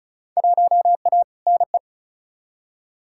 Morse code, heard as a keyed tone, about 36 wpm, 700 Hz. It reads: 1A NE
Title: Field Day Exchanges Encore - 35wpm